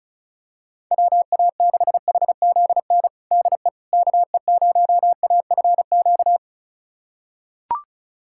Morse code, heard as a keyed tone, about 36 wpm, 690 Hz. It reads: WA6HZN DE KE0AFQ
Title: USA Callsign DE Callsign - 6 Characters - 35wpm